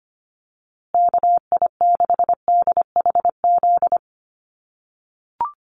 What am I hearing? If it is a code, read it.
Morse code, heard as a keyed tone, about 25 wpm, 700 Hz. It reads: KI6DHZ
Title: USA Call Signs Repeated 2x 25wpm